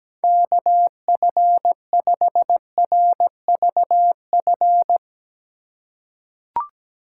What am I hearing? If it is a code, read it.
Morse code, heard as a keyed tone, about 17 wpm, 700 Hz. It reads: KF5RVF
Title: USA Call Signs - 6 Characters - 4x Character Spacing - 17wpm